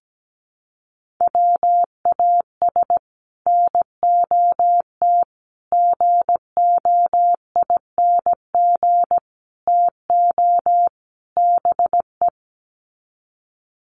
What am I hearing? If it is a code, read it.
Morse code, heard as a keyed tone, about 17 wpm, 700 Hz. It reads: WAS NOT GOING TO BE
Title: Sets of 5 Words Encore 17wpm